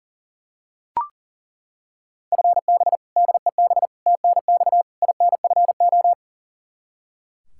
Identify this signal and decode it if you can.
Morse code, heard as a keyed tone, about 40 wpm, 700 Hz. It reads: FB DEB TNX INFO
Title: K1USN SST QSO Practice 40wpm